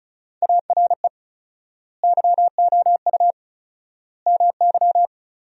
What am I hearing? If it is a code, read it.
Morse code, heard as a keyed tone, about 34 wpm, 700 Hz. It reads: ARE YOU MY
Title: 4x Word-Spacing - Sentences from Top 300 Words - 35wpm